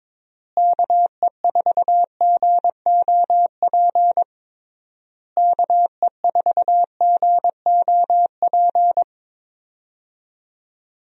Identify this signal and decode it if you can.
Morse code, heard as a keyed tone, about 22 wpm, 700 Hz. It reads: KE4GOP KE4GOP
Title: USA Call Signs - 6 Characters - Repeated 3x - 22wpm